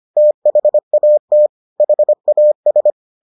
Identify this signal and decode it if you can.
Morse code, heard as a keyed tone, about 25 wpm, 600 Hz. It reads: THAT HAS